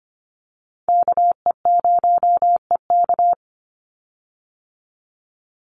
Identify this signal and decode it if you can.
Morse code, heard as a keyed tone, about 25 wpm, 700 Hz. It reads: KE0EK
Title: Call Signs - 25wpm